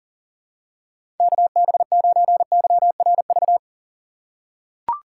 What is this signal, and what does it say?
Morse code, heard as a keyed tone, about 40 wpm, 700 Hz. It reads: KB9YRV
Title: USA Call Signs - 6 Characters - 2x Character Spacing - 40wpm